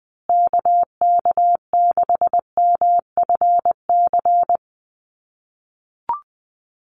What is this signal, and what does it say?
Morse code, heard as a keyed tone, about 20 wpm, 700 Hz. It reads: KK6MFC
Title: USA Call Signs Repeated 2x 20wpm